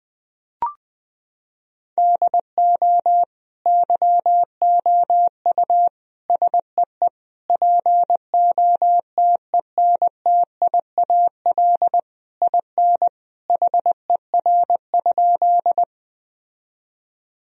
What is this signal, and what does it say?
Morse code, heard as a keyed tone, about 20 wpm, 700 Hz. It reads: DO YOU SEE POTENTIAL IN HER?
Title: Sentences from Top 1000 Words - Review Part 4 - 20wpm